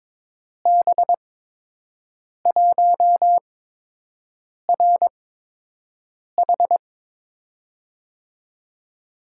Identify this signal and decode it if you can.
Morse code, heard as a keyed tone, about 22 wpm, spaced as for 6 wpm, 700 Hz. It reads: B1RH
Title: USA Call Signs - 5 Characters - 8x Character Spacing - 22wpm